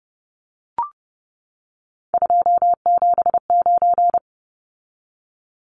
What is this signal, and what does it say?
Morse code, heard as a keyed tone, about 30 wpm, 700 Hz. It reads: 279
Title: Numbers - 3-Digit 30wpm